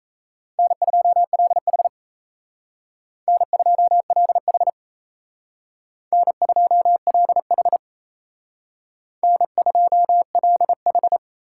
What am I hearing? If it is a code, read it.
Morse code, heard as a keyed tone, about 36 wpm, 700 Hz. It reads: N2LH N2LH N2LH N2LH